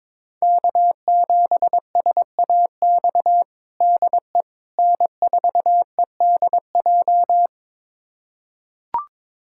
Morse code, heard as a keyed tone, about 22 wpm, 700 Hz. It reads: K7SAX DE N4EDJ
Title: USA Callsign DE Callsign - 5 Characters - 22wpm